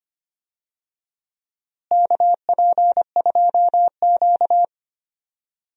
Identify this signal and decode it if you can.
Morse code, heard as a keyed tone, about 25 wpm, 700 Hz. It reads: KP2Q
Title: DXCC-100 Call Signs - 4 Characters - Repeated 2x - 25wpm